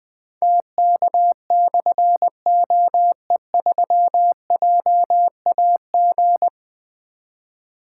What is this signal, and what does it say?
Morse code, heard as a keyed tone, about 20 wpm, 700 Hz. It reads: TK/OE3JAG
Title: Difficult Call Signs 20wpm